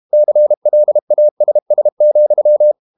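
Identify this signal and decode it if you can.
Morse code, heard as a keyed tone, about 32 wpm, 600 Hz. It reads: CLASS,
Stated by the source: Morse code practice at 32 wpm from CW med Gnister, reading Ken Follett's novel Never in English.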